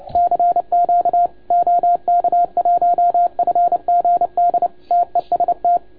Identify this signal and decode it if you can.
Morse code, heard as a keyed tone, about 29 wpm, 690 Hz. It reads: CQ OK1FGD TEST
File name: Archiv / CUC287 / OK1FGD CUC 287